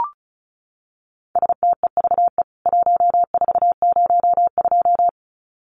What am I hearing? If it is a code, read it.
Morse code, heard as a keyed tone, about 34 wpm, 700 Hz. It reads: STEVE 1402
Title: CWT Exchange 35wpm